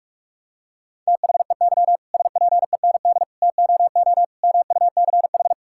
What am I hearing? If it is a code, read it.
Morse code, heard as a keyed tone, about 41 wpm, 700 Hz. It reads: THEY SPEND TOO MUCH
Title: Sentences from Top 1000 Words - Review Part 5 - 45wpm